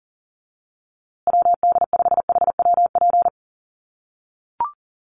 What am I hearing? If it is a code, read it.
Morse code, heard as a keyed tone, about 37 wpm, 700 Hz. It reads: WD5HWP
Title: USA Call Signs Repeated 2x 40wpm